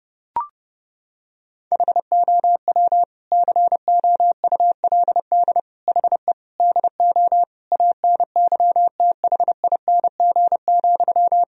Morse code, heard as a keyed tone, about 30 wpm, 700 Hz. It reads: HOW COULD HE DO ANYTHING,
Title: Sentences from Top 1000 Words - Review Part 1 - 30wpm